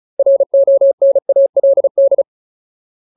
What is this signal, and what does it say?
Morse code, heard as a keyed tone, about 34 wpm, 550 Hz. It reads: RONALD
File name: Ultimate Word List / 35WPM Ultimate